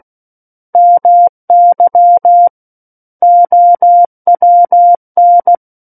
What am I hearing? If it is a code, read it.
Morse code, heard as a keyed tone, about 16 wpm, 700 Hz. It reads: MY OWN
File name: Book One / 07-Chapter 7 - 16 WPM